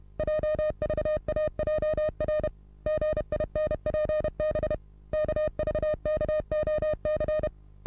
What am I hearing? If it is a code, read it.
Morse code, heard as a keyed tone, about 29 wpm, 610 Hz. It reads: JVAJR GINPB KVKOC
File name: repos / Morsekurs / DJ4UF Morsen27